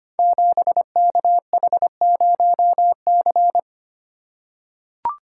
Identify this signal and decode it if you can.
Morse code, heard as a keyed tone, about 25 wpm, 700 Hz. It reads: 7KH0C
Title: Code Groups - 5 Letter:Numbers - 25wpm